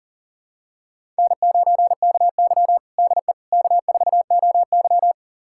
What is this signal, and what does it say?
Morse code, heard as a keyed tone, about 40 wpm, 700 Hz. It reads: N9KY DE K4OY
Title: USA Callsign DE Callsign - 4 Characters - 40wpm